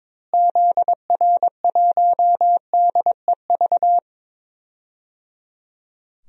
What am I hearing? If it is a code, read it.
Morse code, heard as a keyed tone, about 22 wpm, 700 Hz. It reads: ZR1DEV
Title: DXCC-100 Call Signs - 6 Characters - Repeated 3x - 22wpm